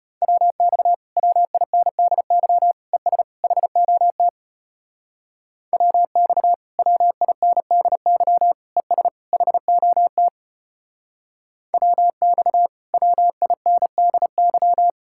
Morse code, heard as a keyed tone, about 33 wpm, 700 Hz. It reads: WX WINDY ES HOT WX WINDY ES HOT WX WINDY